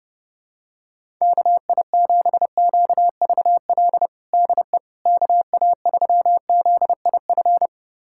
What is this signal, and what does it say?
Morse code, heard as a keyed tone, about 32 wpm, 700 Hz. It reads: KI7QVL DE KA3ZIF
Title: USA Callsign DE Callsign 30wpm